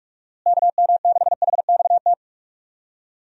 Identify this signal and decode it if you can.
Morse code, heard as a keyed tone, about 46 wpm, 700 Hz. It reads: KM6HXT
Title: USA Call Signs - 6 Characters - Repeated 2x - 45wpm